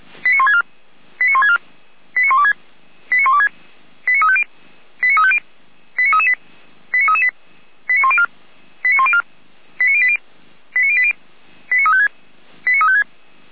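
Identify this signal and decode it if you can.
Five-tone selective call (ZVEI), heard as quick runs of five tones, tones of about 70 ms. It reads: 89164 89164 89126 89126 89350 89350 89338 89338 89104 89104 89080 89080 89356 89356
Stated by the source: ZVEI five-tone selective call received via SDR.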